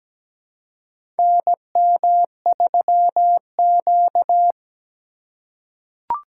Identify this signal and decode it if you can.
Morse code, heard as a keyed tone, about 17 wpm, 700 Hz. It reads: NM3Q
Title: USA Call Signs - 4 Characters - 3x Character Spacing - 17wpm